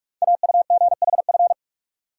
Morse code, heard as a keyed tone, about 44 wpm, 700 Hz. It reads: AUGHF
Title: Code Groups - 5 Letters - 45wpm